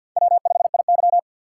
Morse code, heard as a keyed tone, about 48 wpm, 700 Hz. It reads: W5IY
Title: USA Call Signs - 4 Characters - 6x Character Spacing - 50wpm